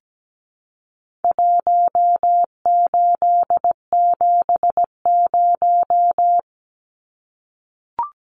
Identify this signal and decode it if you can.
Morse code, heard as a keyed tone, about 17 wpm, 700 Hz. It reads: 1870
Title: Numbers - Contest Style - 4-Digit 17wpm